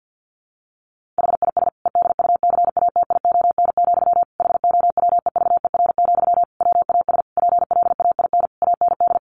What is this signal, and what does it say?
Morse code, heard as a keyed tone, about 50 wpm, 700 Hz. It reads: HIS EDUCATION, HOWEVER, WAS PLAIN AND
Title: Sentences from Top 1400 Words 50wpm